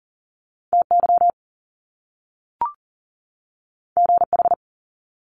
Morse code, heard as a keyed tone, about 39 wpm, 700 Hz. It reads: TY GH
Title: Top Two Letter Combinations in Words 40wpm